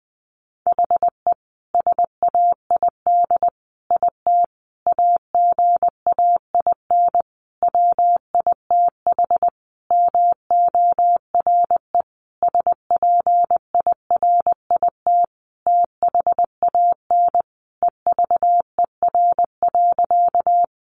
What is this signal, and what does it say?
Morse code, heard as a keyed tone, about 20 wpm, 700 Hz. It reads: HE SAID IT AGAIN WITH MORE SPIRIT THAN EVER.
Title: Sentences from Top 1400 Words 20wpm